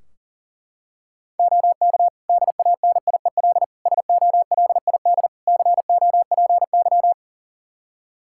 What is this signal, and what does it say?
Morse code, heard as a keyed tone, about 40 wpm, 700 Hz. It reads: OK DANIEL SOLID COPY
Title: K1USN SST QSO Practice 40wpm